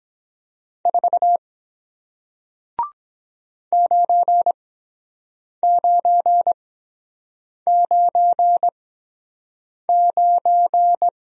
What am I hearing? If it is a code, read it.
Morse code, heard as a keyed tone, about 23 wpm, spaced as for 7 wpm, 700 Hz. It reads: 4 9999